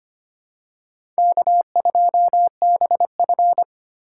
Morse code, heard as a keyed tone, about 25 wpm, 700 Hz. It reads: K2BF
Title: USA Call Signs - 3x Character Spacing - 25wpm